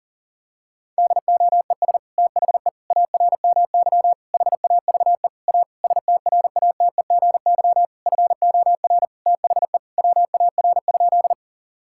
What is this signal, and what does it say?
Morse code, heard as a keyed tone, about 40 wpm, 700 Hz. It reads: DOES THE ARMY HAVE A STRATEGY FOR THE WAR?
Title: Sentences from Top 1000 Words - Review Part 5 - 40wpm